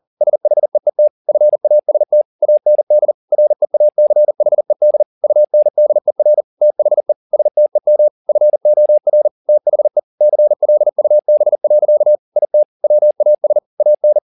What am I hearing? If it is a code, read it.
Morse code, heard as a keyed tone, about 40 wpm, 600 Hz. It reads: SHEET FAST AND REACHED UNDER THE STEM FOR THE CLUB. IT WAS AN
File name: the old man and the sea 2 40wpm